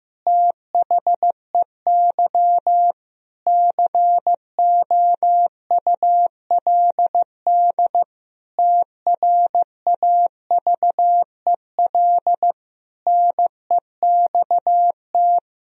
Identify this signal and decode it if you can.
Morse code, heard as a keyed tone, about 15 wpm, 700 Hz. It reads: THEY COULD TRAVEL NEXT
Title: Sentences from Top 1000 Words - Review Part 2 - 15wpm